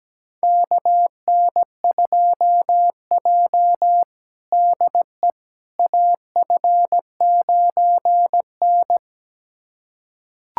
Morse code, heard as a keyed tone, about 17 wpm, 700 Hz. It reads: KN2J DE AF9N
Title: USA Callsign DE Callsign - 4 Characters - 17wpm